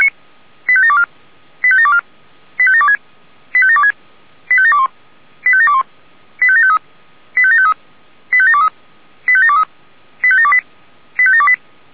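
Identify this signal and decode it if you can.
Five-tone selective call (ZVEI), heard as quick runs of five tones, tones of about 70 ms. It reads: 85724 85724 85726 85726 85721 85721 85753 85753 85723 85723 85728 85728
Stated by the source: Five-tone ZVEI selective call received by SDR.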